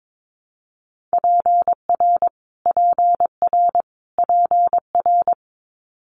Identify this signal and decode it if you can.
Morse code, heard as a keyed tone, about 22 wpm, 700 Hz. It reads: PR PR PR
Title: Multi-letter Phonemes Repeated 3x 22wpm